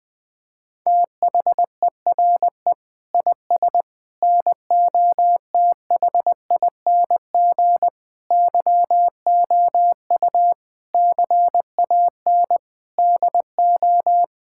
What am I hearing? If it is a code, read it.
Morse code, heard as a keyed tone, about 19 wpm, 700 Hz. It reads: THERE IS NOTHING YOU CAN DO
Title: Sentences with 7-Word Phrases - Spoken Beginning - 20wpm